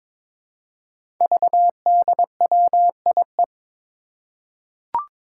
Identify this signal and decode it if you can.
Morse code, heard as a keyed tone, about 22 wpm, 700 Hz. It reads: VDWIE
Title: Five Challenging Letter-Number Combinations 22wpm